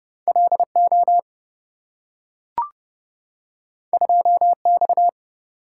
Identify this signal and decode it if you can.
Morse code, heard as a keyed tone, about 27 wpm, 700 Hz. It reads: LO 2X